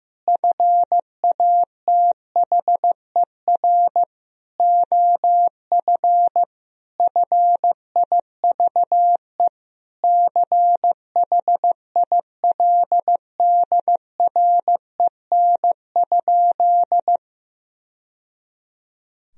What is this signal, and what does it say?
Morse code, heard as a keyed tone, about 15 wpm, 700 Hz. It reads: FATHER OF FIVE CHILDREN?
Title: Sentences from Top 1000 Words - Review Part 2 - 15wpm